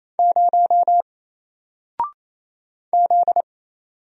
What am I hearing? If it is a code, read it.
Morse code, heard as a keyed tone, about 27 wpm, 700 Hz. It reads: 0 Z
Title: Single Letter-Number 28wpm